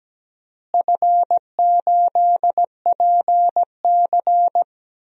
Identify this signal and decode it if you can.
Morse code, heard as a keyed tone, about 17 wpm, 700 Hz. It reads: F8PC
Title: DXCC-100 Call Signs - 4 Characters - Repeated 3x - 17wpm